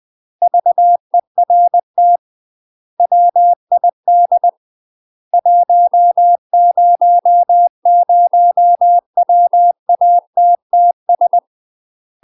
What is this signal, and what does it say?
Morse code, heard as a keyed tone, about 20 wpm, 700 Hz. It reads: VERT WID 100WATTS
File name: UK MASTER QSOs TRAINING / 20WPM / UK MASTER QSOs 0042